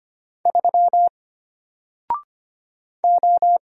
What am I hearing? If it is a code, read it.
Morse code, heard as a keyed tone, about 25 wpm, 700 Hz. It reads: 3 O
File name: Single-Letter-Number-25wpm